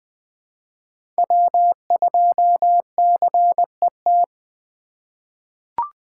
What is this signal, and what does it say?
Morse code, heard as a keyed tone, about 20 wpm, 700 Hz. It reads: W2CET
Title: USA Call Signs - 5 Characters - 2x Character Spacing - 20wpm